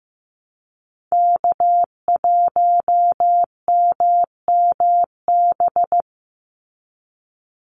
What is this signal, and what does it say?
Morse code, heard as a keyed tone, about 15 wpm, 700 Hz. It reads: K1MMB
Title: Call Signs Repeated 2x - 15wpm